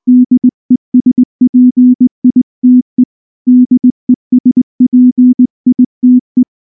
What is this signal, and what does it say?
Morse code, heard as a keyed tone, about 19 wpm, 250 Hz. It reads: DESPITE DESPITE
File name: morse / morse-training-20WPM